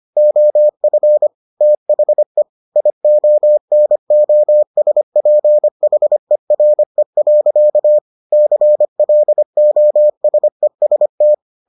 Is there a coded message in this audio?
Yes: Morse code from 0.2 to 11.3 s, heard as a keyed tone, about 25 wpm, 600 Hz. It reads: OF THE IONOSPHERE. CLOSEST